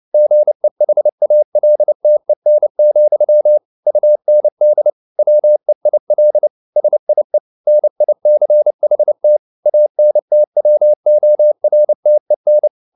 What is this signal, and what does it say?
Morse code, heard as a keyed tone, about 29 wpm, 600 Hz. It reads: GEHALTEN, UND WEIL SIE NICHT ANTWORTEN